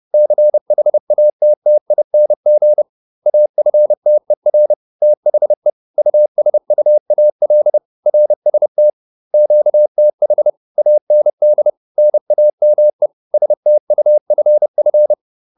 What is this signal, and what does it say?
Morse code, heard as a keyed tone, about 30 wpm, 600 Hz. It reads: CHATTING AFTER THE USUAL RST QTH AND NAME STUFF